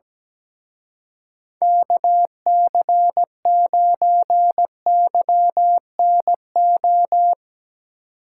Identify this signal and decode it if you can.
Morse code, heard as a keyed tone, about 19 wpm, 700 Hz. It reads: KC9YNO
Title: Speed Racing - USA 2x3 Call Signs 17wpm